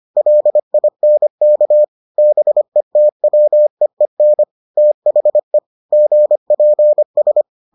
Morse code, heard as a keyed tone, about 25 wpm, 600 Hz. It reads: LINK BETWEEN THE GPS